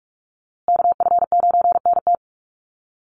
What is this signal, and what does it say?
Morse code, heard as a keyed tone, about 44 wpm, 700 Hz. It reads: KF9NT
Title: USA Call Signs Repeated 2x 45wpm